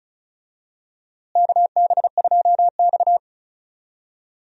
Morse code, heard as a keyed tone, about 34 wpm, 700 Hz. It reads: KB2X
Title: USA Call Signs - 4 Characters - Repeated 3x - 35wpm